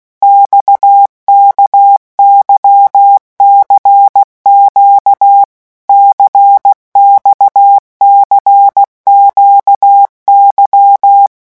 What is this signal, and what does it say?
Morse code, heard as a keyed tone, about 16 wpm, 800 Hz. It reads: XKYCQ CXCQY
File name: morse mp3 / revision1-vitesse 15 mots